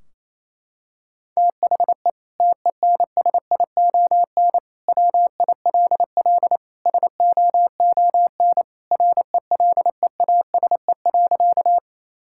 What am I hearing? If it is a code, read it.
Morse code, heard as a keyed tone, about 28 wpm, 700 Hz. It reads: THE TENSION WILL SOON RELEASE.
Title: Sentences from Top 1600 Words 28wpm